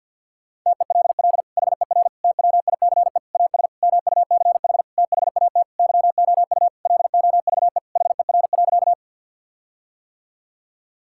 Morse code, heard as a keyed tone, about 51 wpm, 700 Hz. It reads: TELL HER TWICE AS MUCH THAT YOU LOVE HER.
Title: Sentences from Top 1600 Words 50wpm